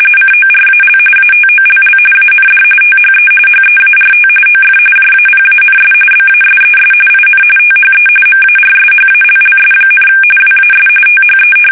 RTTY, about 45.45 baud. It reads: CORAL ATOLL AND THE HEART OF JAPANS BASES IN THE MARSHALLS, WAS THE
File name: fsk850 / battleshipnj